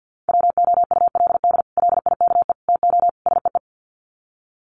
Morse code, heard as a keyed tone, about 50 wpm, 700 Hz. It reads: WOULD LIKE TO SEE
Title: Sets of 4 Words 50wpm